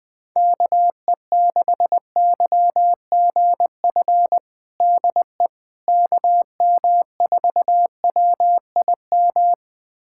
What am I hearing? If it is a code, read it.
Morse code, heard as a keyed tone, about 21 wpm, 700 Hz. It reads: KE6YGF DE KM4WIM
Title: USA Callsign DE Callsign - 6 Characters - 20wpm